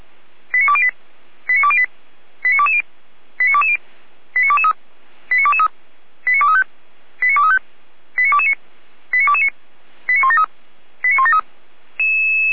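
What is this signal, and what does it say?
Five-tone selective call (ZVEI), heard as quick runs of five tones, tones of about 70 ms. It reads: 89208 89208 89220 89220 89223 89223 89235 89235 89229 89229 89173 89173
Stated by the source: ZVEI five-tone call 89169 received by SDR.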